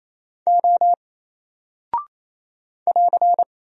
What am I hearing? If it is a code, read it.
Morse code, heard as a keyed tone, about 27 wpm, 700 Hz. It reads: O <AR>
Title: Single Character (Letter:Number:Punctuation:Prosign) 28wpm